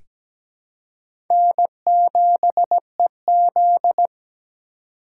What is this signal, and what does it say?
Morse code, heard as a keyed tone, about 17 wpm, 700 Hz. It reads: N7EZ
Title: USA Call Signs - 4 Characters - 3x Character Spacing - 17wpm